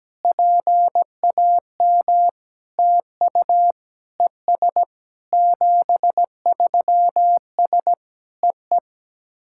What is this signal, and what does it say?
Morse code, heard as a keyed tone, about 17 wpm, 700 Hz. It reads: PAM TU ES 73S EE
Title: SOTA QSO Practice 17wpm